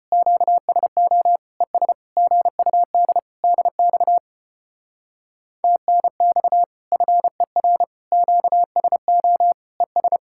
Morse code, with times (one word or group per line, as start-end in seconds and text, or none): 0.1-1.4 QSO
1.6-1.9 ES
2.2-3.2 GUD
3.4-4.2 DX
5.6-6.6 TNX
6.9-7.8 FER
8.1-9.5 QSO
9.8-10.2 ES